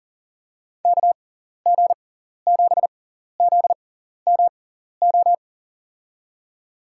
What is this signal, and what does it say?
Morse code, heard as a keyed tone, about 40 wpm, spaced as for 14 wpm, 700 Hz. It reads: KG7ZMO
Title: USA Call Signs - 6 Characters - 6x Character Spacing - 40wpm